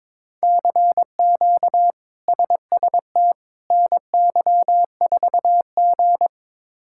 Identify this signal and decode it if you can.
Morse code, heard as a keyed tone, about 22 wpm, 700 Hz. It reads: CQ SST NY4G
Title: K1USN SST QSO Practice - 22wpm